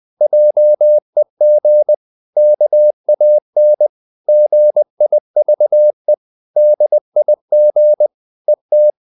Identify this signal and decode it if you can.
Morse code, heard as a keyed tone, about 20 wpm, 600 Hz. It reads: JEG KAN GIVE DIG ET
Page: Ferie 20 wpm